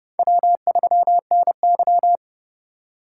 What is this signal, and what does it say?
Morse code, heard as a keyed tone, about 37 wpm, 700 Hz. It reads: W3NY
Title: Speed Racing - USA Call Signs 30wpm